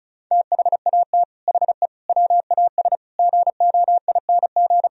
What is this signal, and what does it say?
Morse code, heard as a keyed tone, about 44 wpm, 700 Hz. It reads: THAT HE WAS GOING
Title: Speed-Racing - Sets of 5 Words 35wpm